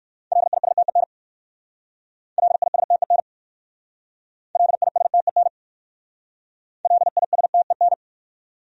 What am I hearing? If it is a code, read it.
Morse code, heard as a keyed tone, about 60 wpm, 690 Hz. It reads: LISTEN LISTEN LISTEN LISTEN